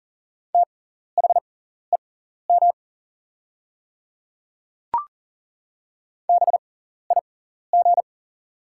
Morse code, heard as a keyed tone, about 40 wpm, spaced as for 14 wpm, 700 Hz. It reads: THEM BIG
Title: 6x Character Spacing - Top 300 Words - 40wpm